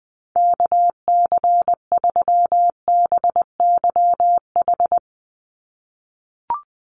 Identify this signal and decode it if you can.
Morse code, heard as a keyed tone, about 19 wpm, 700 Hz. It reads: KC3BYH
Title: USA Call Signs 20wpm